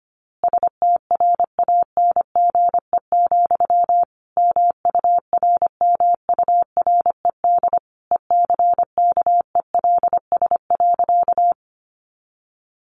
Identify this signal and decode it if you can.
Morse code, heard as a keyed tone, about 25 wpm, 700 Hz. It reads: STRANGE, MURMURED ECKELS.
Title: A Sound Of Thunder 25wpm